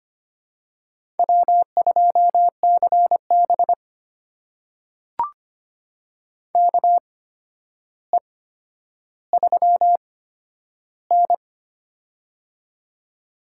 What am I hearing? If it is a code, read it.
Morse code, heard as a keyed tone, about 25 wpm, spaced as for 11 wpm, 700 Hz. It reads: W2CB KE3N
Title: USA Call Signs - 4 Characters - 8x Character Spacing - 25wpm